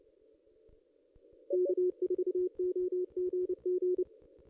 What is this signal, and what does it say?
Morse code, heard as a keyed tone, about 29 wpm, 360 Hz. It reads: K4OGG
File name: contests / ss-cw-k5tr-op-n5rz -1900